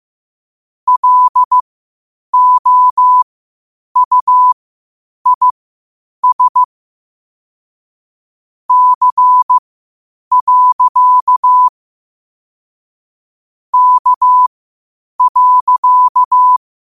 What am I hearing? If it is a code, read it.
Morse code, heard as a keyed tone, about 15 wpm, spaced as for 8 wpm, 1000 Hz. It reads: LOUIS C. K.
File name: qotdfiles / 8wpm